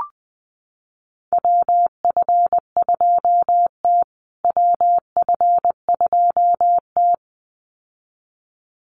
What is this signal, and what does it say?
Morse code, heard as a keyed tone, about 20 wpm, 700 Hz. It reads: WF2T WF2T
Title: USA Call Signs Repeated 2x 20wpm